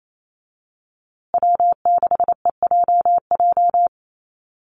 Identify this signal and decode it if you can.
Morse code, heard as a keyed tone, about 28 wpm, 700 Hz. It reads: W6EJJ
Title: USA Call Signs Repeated 3x 28wpm